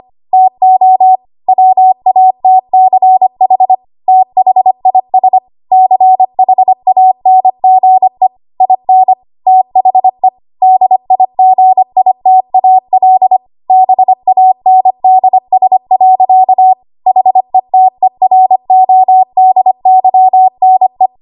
Morse code, heard as a keyed tone, about 25 wpm, 750 Hz. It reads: TO WATCH THIS CHANGE IN THE DIGITAL BANDS. HETERODYNE
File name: Archive / 25 WPM / 25WPM